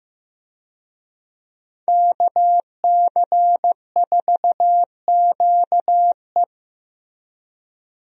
Morse code, heard as a keyed tone, about 15 wpm, 700 Hz. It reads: KC4QE
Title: USA Call Signs - 5 Characters - 15wpm